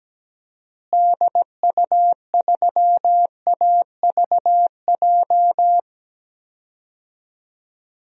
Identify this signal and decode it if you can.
Morse code, heard as a keyed tone, about 17 wpm, 700 Hz. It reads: DU3AVJ
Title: DXCC-100 Call Signs - 6 Characters - Repeated 2x - 17wpm